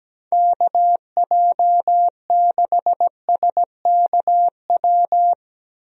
Morse code, heard as a keyed tone, about 17 wpm, 700 Hz. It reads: KJ6SKW
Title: USA Call Signs - 6 Characters - Repeated 3x - 17wpm